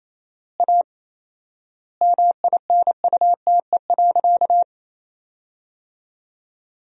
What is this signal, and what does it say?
Morse code, heard as a keyed tone, about 28 wpm, 700 Hz. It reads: A MINUTE.
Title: 4x Word-Spacing - Sentences from Top 500 Words - 28wpm